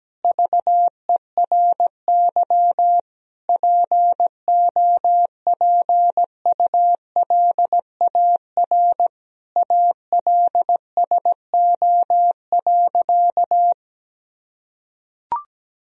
Morse code, heard as a keyed tone, about 17 wpm, 700 Hz. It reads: VERY POPULAR ALSO.
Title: Sentences from Top 1000 Words - Review Part 3 - 17wpm